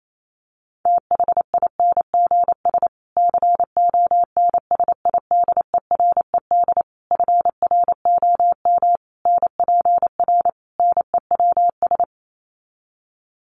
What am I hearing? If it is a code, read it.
Morse code, heard as a keyed tone, about 28 wpm, 700 Hz. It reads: THINGS CONSIDERED FROM NPR NEWS
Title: Sets of 5 Words Encore 28wpm